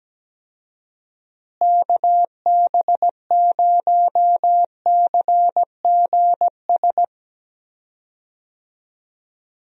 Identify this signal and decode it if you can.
Morse code, heard as a keyed tone, about 17 wpm, 700 Hz. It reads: KB0CGS
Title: USA Call Signs - 6 Characters - Repeated 2x - 17wpm